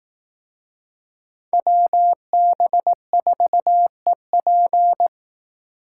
Morse code, heard as a keyed tone, about 18 wpm, 700 Hz. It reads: WB4EP
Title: Speed Racing - USA 2x2 Call Signs 15wpm